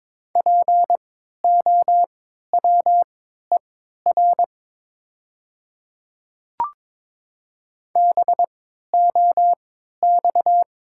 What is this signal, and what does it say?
Morse code, heard as a keyed tone, about 22 wpm, spaced as for 12 wpm, 700 Hz. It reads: POWER BOX
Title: 3x Character Spacing - Top 500 Words - 22wpm